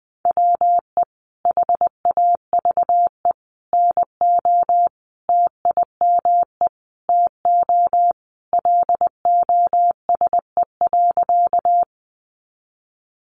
Morse code, heard as a keyed tone, about 20 wpm, 700 Hz. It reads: WE HAVE NO TIME TO LOSE.
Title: Sentences from Top 1100 Words 20wpm